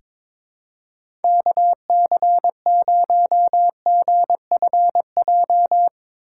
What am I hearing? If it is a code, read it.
Morse code, heard as a keyed tone, about 22 wpm, 700 Hz. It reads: KC0GFJ
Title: USA Call Signs - 6 Characters - 4x Character Spacing - 22wpm